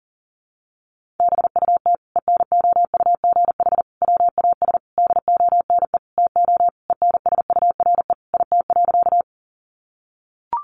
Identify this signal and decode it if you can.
Morse code, heard as a keyed tone, about 40 wpm, 700 Hz. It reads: BUT ENOUGH WAS DONE TO ENSURE IT.
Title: Sentences from Top 1100 Words 40wpm